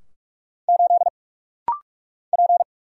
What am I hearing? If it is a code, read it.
Morse code, heard as a keyed tone, about 46 wpm, 700 Hz. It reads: 8 P
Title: Single Letter-Number Alt - Rapid-Fire 45wpm